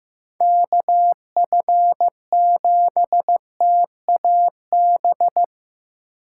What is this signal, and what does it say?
Morse code, heard as a keyed tone, about 15 wpm, 700 Hz. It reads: KF7TAB
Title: USA Call Signs - 4x Character Spacing - 15wpm